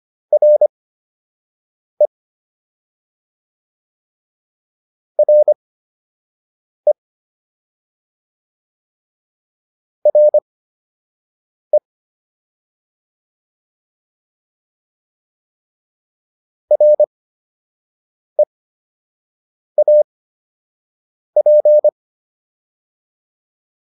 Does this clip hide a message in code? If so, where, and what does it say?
Yes: Morse code from 0.3 to 21.9 s, heard as a keyed tone, about 25 wpm, spaced as for 6 wpm, 600 Hz. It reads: RE RE RE REAP